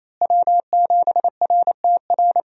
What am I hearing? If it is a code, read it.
Morse code, heard as a keyed tone, about 27 wpm, 700 Hz. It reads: W7RTR
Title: USA Call Signs - 5 Characters - Repeated 3x - 28wpm